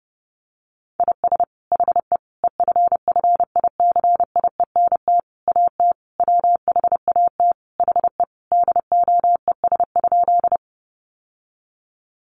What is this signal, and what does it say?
Morse code, heard as a keyed tone, about 30 wpm, 700 Hz. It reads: IS HE EFFICIENT AT WHAT HE DOES?
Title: Sentences from Top 1300 Words 30wpm